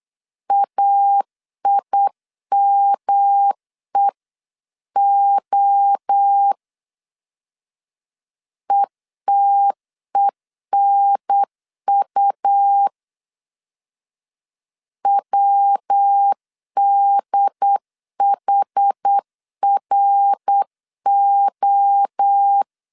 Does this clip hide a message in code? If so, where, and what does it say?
Yes: Morse code from 0.5 to 22.6 s, heard as a keyed tone, about 8 wpm, 800 Hz. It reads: AIMEO ETENU WDHRO